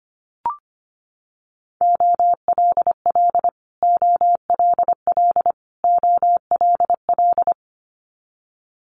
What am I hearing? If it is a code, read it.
Morse code, heard as a keyed tone, about 25 wpm, 700 Hz. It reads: OLL OLL OLL
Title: Multi-letter Phonemes Repeated 3x 25wpm